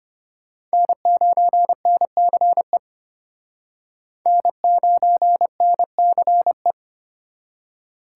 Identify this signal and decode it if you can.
Morse code, heard as a keyed tone, about 33 wpm, 700 Hz. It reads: N9NCE N9NCE